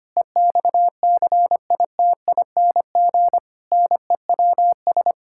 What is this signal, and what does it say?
Morse code, heard as a keyed tone, about 25 wpm, 700 Hz. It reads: EXCITING NEWS
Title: Pangram Challenge v2 - 25wpm